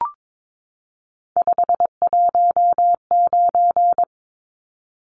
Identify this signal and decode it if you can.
Morse code, heard as a keyed tone, about 22 wpm, 700 Hz. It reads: 519
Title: Numbers - 3-Digit 22wpm